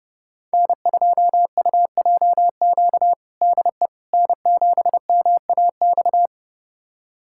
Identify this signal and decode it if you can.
Morse code, heard as a keyed tone, about 30 wpm, 700 Hz. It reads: N2UJQ DE N7MAX
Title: USA Callsign DE Callsign - 5 Characters - 30wpm